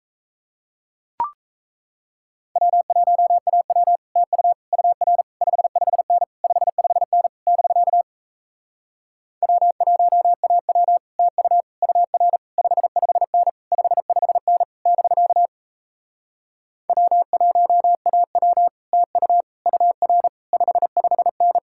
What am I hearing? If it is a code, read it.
Morse code, heard as a keyed tone, about 38 wpm, 700 Hz. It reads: W1AW TU UR 55N 55N <BK> W1AW TU UR 55N 55N <BK> W1AW TU UR 55N